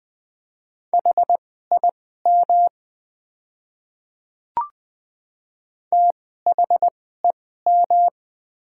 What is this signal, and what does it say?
Morse code, heard as a keyed tone, about 20 wpm, spaced as for 15 wpm, 700 Hz. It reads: HIM THEM
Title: 2x Character Spacing - Top 200 Words - 20wpm